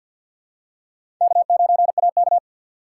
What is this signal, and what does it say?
Morse code, heard as a keyed tone, about 52 wpm, 700 Hz. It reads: K9AK
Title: USA Call Signs - 4 Characters - 6x Character Spacing - 50wpm